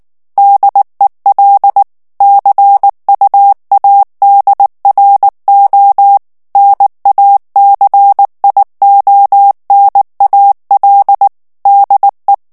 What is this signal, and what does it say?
Morse code, heard as a keyed tone, about 19 wpm, 800 Hz. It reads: DEL CUADRO NACIONAL DE
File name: morse mp3 / ESPAGNOL TEXTE EN CLAIR 2 VITESSE 18 MOTS MINUTES